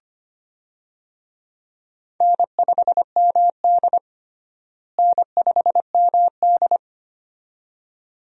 Morse code, heard as a keyed tone, about 25 wpm, 700 Hz. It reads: N5MD N5MD
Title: USA Call Signs - 4 Characters - Repeated 2x - 25wpm